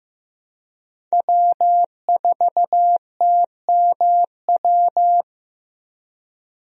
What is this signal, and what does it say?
Morse code, heard as a keyed tone, about 15 wpm, 700 Hz. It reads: W4TMW
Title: USA Call Signs - 5 Characters - Repeated 2x - 15wpm